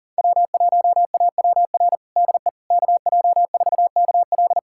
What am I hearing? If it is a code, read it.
Morse code, heard as a keyed tone, about 40 wpm, 700 Hz. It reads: W1AWR DE KJ4KL
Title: USA Callsign DE Callsign - 5 Characters - 40wpm